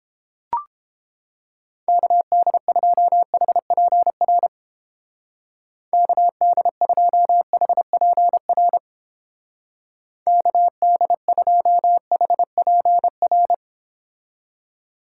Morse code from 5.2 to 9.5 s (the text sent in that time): KD2HPR